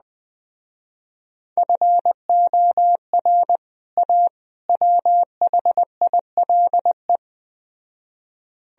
Morse code, heard as a keyed tone, about 20 wpm, 700 Hz. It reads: FOR A WHILE